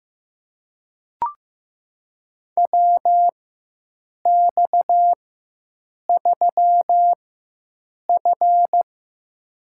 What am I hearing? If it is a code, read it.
Morse code, heard as a keyed tone, about 15 wpm, spaced as for 7 wpm, 700 Hz. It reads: WX3F
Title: USA Call Signs - 4x Character Spacing - 15wpm